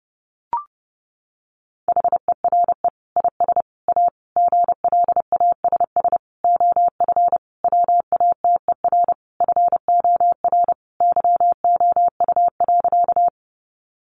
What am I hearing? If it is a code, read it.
Morse code, heard as a keyed tone, about 30 wpm, 700 Hz. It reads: HERE IS A GLASS OF WATER FOR YOU.
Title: Sentences from Top 1300 Words 30wpm